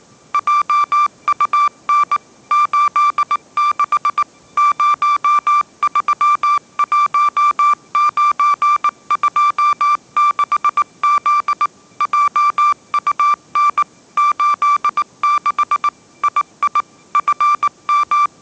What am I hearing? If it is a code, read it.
Morse code, heard as a keyed tone, about 19 wpm, 1180 Hz. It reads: JUN 86 031926Z JUN 86 II FM
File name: mrx083 20wpm